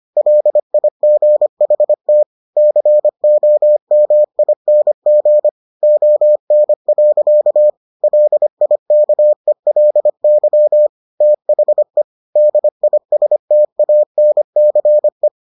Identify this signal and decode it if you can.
Morse code, heard as a keyed tone, about 25 wpm, 600 Hz. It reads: LIGHT COMING ON. LIKELY THE DISTANCE